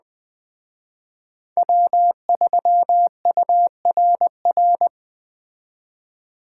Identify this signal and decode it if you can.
Morse code, heard as a keyed tone, about 20 wpm, 700 Hz. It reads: W3URR